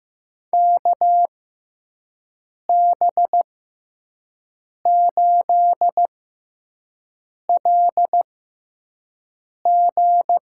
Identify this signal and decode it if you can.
Morse code, heard as a keyed tone, about 15 wpm, spaced as for 5 wpm, 700 Hz. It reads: KB8LG
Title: USA Call Signs - 6x Character Spacing - 15wpm